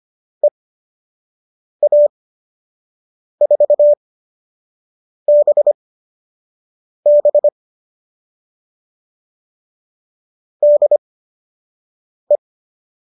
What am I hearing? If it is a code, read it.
Morse code, heard as a keyed tone, about 25 wpm, spaced as for 6 wpm, 590 Hz. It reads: EA4BB DE